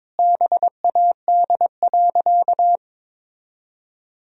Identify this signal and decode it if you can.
Morse code, heard as a keyed tone, about 22 wpm, 700 Hz. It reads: BAD.
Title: Sentences from Top 1000 Words - Review Part 3 - 22wpm